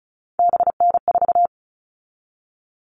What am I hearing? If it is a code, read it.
Morse code, heard as a keyed tone, about 34 wpm, 700 Hz. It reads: BN4
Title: All Cut Numbers - 3-Digit 35wpm